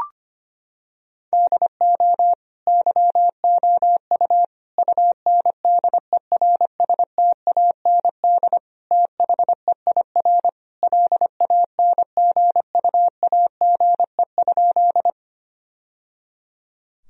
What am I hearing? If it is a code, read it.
Morse code, heard as a keyed tone, about 25 wpm, 700 Hz. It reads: DO YOU UNDERSTAND THEIR LANGUAGE?
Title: Sentences from Top 1000 Words - Review Part 3 - 25wpm